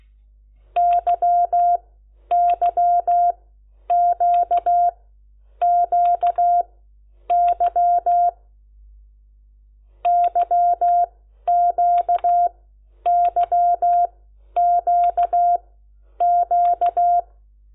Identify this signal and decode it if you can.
Morse code, heard as a keyed tone, about 16 wpm, spaced as for 10 wpm, 700 Hz. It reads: YYQQY YQYQQ
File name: Lecciones / CLASE Nº10